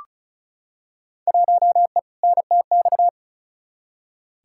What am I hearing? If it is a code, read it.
Morse code, heard as a keyed tone, about 34 wpm, 700 Hz. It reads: 1E NTX
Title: Field Day Exchanges - 35wpm